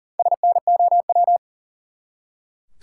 Morse code, heard as a keyed tone, about 40 wpm, 700 Hz. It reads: SNOW
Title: Four Letter Words - 40wpm